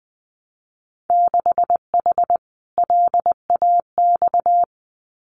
Morse code, heard as a keyed tone, about 20 wpm, 700 Hz. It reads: 6H LAX
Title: Winter Field Day Exchanges Encore 20wpm